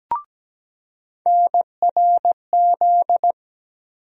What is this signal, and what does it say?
Morse code, heard as a keyed tone, about 17 wpm, 700 Hz. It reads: NRZ
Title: USA Call Sign Postfixes - 3 Letters 17wpm